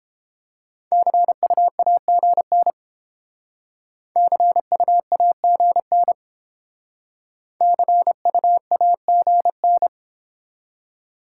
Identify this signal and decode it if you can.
Morse code, heard as a keyed tone, about 31 wpm, 690 Hz. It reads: CUAGN CUAGN CUAGN